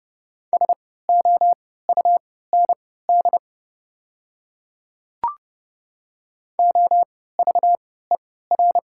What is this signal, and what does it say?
Morse code, heard as a keyed tone, about 30 wpm, spaced as for 17 wpm, 690 Hz. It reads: SOUND OVER
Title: 3x Character Spacing - Top 100 Words - 30wpm